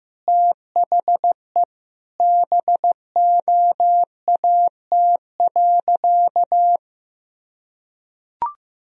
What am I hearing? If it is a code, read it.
Morse code, heard as a keyed tone, about 15 wpm, 700 Hz. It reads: THE BOAT.
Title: Sentences from Top 1000 Words - Review Part 3 - 15wpm